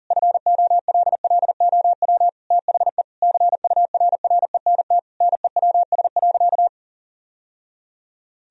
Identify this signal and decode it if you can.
Morse code, heard as a keyed tone, about 37 wpm, 700 Hz. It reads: FOLLOW THE CURRENT NEWS.
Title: Sentences from Top 1000 Words - Review Part 4 - 40wpm